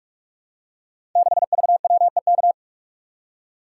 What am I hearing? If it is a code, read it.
Morse code, heard as a keyed tone, about 44 wpm, 700 Hz. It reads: 6VWEK
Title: Code Groups - 5 Letter:Numbers - 45wpm